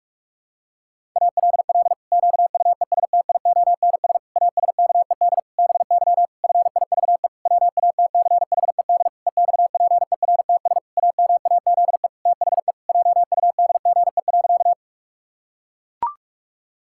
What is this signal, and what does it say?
Morse code, heard as a keyed tone, about 41 wpm, 700 Hz. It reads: ALL QUESTIONS ASKED BY FIVE WATCHED EXPERTS AMAZE THE JUDGE.
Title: Pangram Challenge v2 - 45wpm